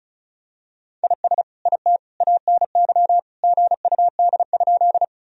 Morse code, heard as a keyed tone, about 36 wpm, 700 Hz. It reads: IS IT ANY GUD?
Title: Sentences from Top 200 Words and Abbreviations 35wpm